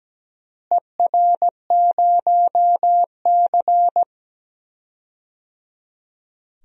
Morse code, heard as a keyed tone, about 17 wpm, 700 Hz. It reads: ER0C
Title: DXCC-100 Call Signs - 4 Characters - Repeated 3x - 17wpm